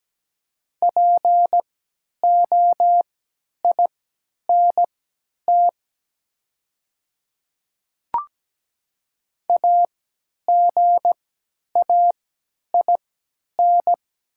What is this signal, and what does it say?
Morse code, heard as a keyed tone, about 17 wpm, spaced as for 10 wpm, 700 Hz. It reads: POINT AGAIN
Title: 3x Character Spacing - Top 200 Words - 17wpm